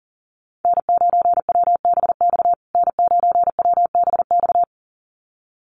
Morse code, heard as a keyed tone, about 37 wpm, 700 Hz. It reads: N9WBX N9WBX
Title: USA Call Signs Repeated 2x 40wpm